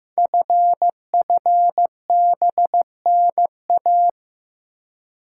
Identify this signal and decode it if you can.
Morse code, heard as a keyed tone, about 15 wpm, 700 Hz. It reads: FFBNA
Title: Code Groups - 5 Letters - 15wpm